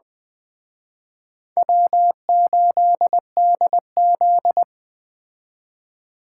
Morse code, heard as a keyed tone, about 21 wpm, 700 Hz. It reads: W8DZ